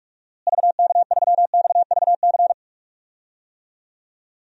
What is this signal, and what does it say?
Morse code, heard as a keyed tone, about 46 wpm, 700 Hz. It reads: VK3XVC
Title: DXCC-100 Call Signs - 6 Characters - Repeated 3x - 45wpm